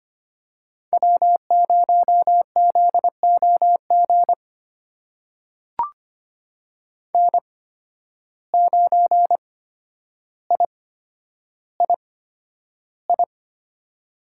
Morse code, heard as a keyed tone, about 25 wpm, spaced as for 11 wpm, 700 Hz. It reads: W0ZOG N9III
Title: USA Call Signs - 5 Characters - 8x Character Spacing - 25wpm